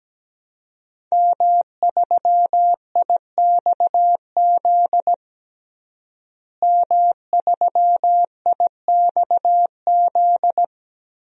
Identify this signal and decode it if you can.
Morse code, heard as a keyed tone, about 17 wpm, 700 Hz. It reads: M3IXZ M3IXZ
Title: DXCC-100 Call Signs - 5 Characters - Repeated 3x - 17wpm